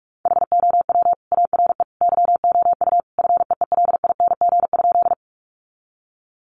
Morse code, heard as a keyed tone, about 44 wpm, 700 Hz. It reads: HOW ARE YOU FEELING?
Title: Sentences from Top 1200 Words 45wpm